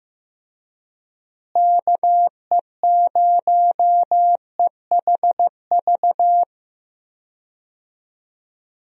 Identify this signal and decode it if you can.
Morse code, heard as a keyed tone, about 15 wpm, 700 Hz. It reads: KE0EHV
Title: USA Call Signs - 6 Characters - Repeated 2x - 15wpm